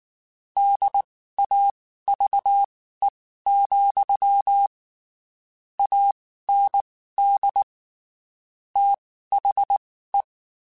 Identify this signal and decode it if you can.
Morse code, heard as a keyed tone, about 19 wpm, spaced as for 14 wpm, 800 Hz. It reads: DAVE, AND THE
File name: (spaced) / QSOs Spaced-mp3s / WA7QOX 2 @18 sp